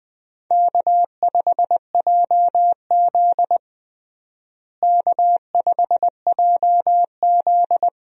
Morse code, heard as a keyed tone, about 20 wpm, 700 Hz. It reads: K5JZ K5JZ
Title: USA Call Signs - 4 Characters - Repeated 2x - 20wpm